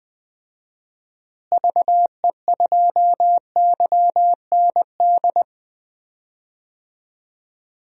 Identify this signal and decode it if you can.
Morse code, heard as a keyed tone, about 20 wpm, 700 Hz. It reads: VE2YND
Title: DXCC-100 Call Signs - 6 Characters - Repeated 3x - 20wpm